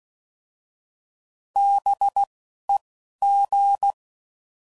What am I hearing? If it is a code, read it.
Morse code, heard as a keyed tone, about 16 wpm, 800 Hz. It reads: B E G